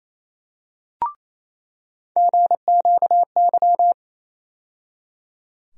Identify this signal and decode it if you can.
Morse code, heard as a keyed tone, about 28 wpm, 700 Hz. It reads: GQY
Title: USA Call Sign Postfixes - 3 Letters 28wpm